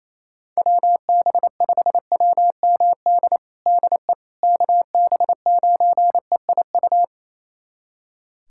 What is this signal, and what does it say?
Morse code, heard as a keyed tone, about 28 wpm, 700 Hz. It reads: WB5WMD DE KB9EIU
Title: USA Callsign DE Callsign - 6 Characters - 28wpm